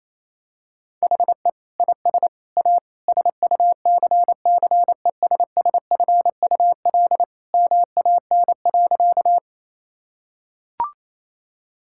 Morse code, heard as a keyed tone, about 28 wpm, 700 Hz. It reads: HE IS A SUCCESSFUL MAN.
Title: Sentences from Top 1000 Words - Review Part 6 - 28wpm